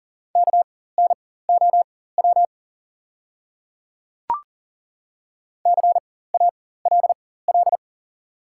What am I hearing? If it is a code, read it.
Morse code, heard as a keyed tone, about 37 wpm, spaced as for 18 wpm, 700 Hz. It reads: KNOW CALL
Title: 4x Character Spacing - Top 100 Words - 40wpm